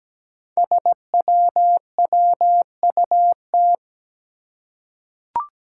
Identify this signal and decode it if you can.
Morse code, heard as a keyed tone, about 17 wpm, 700 Hz. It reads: SWWUT
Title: Five Challenging Letter-Number Combinations 17wpm